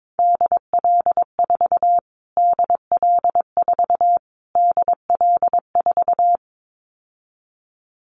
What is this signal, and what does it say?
Morse code, heard as a keyed tone, about 22 wpm, 700 Hz. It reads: DL4 DL4 DL4
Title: Callsign Prefixes for Top 25 DXCC - Repeated 3x - 22wpm